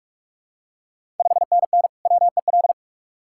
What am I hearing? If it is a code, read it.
Morse code, heard as a keyed tone, about 48 wpm, 690 Hz. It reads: 5NN WEL
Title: Ontario QSO Party Exchange 45wpm